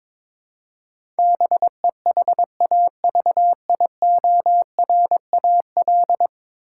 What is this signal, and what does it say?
Morse code, heard as a keyed tone, about 22 wpm, 700 Hz. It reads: BEHAVIORAL
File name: practice / 22wpm / 10 Letter Words - First 3 Letters Given - 22wpm